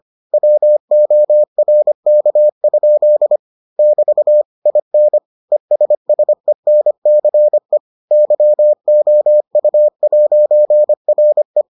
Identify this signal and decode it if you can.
Morse code, heard as a keyed tone, about 25 wpm, 600 Hz. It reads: WORK? <BT> IN ESSENCE YOU'RE